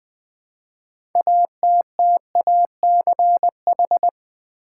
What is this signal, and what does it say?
Morse code, heard as a keyed tone, about 20 wpm, 700 Hz. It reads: ATTACH
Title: Build A Word - 3 Letters At A Time - 6-Letter Words - 20wpm